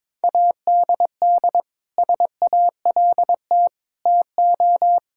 Morse code, heard as a keyed tone, about 22 wpm, 700 Hz. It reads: ADD SALT TO
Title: Sentences from Top 1000 Words - Review Part 5 - 22wpm